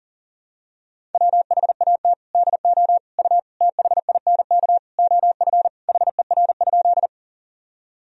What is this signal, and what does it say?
Morse code, heard as a keyed tone, about 39 wpm, 700 Hz. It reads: WHAT DO U THINK OF HER?
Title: Sentences from Top 200 Words and Abbreviations 40wpm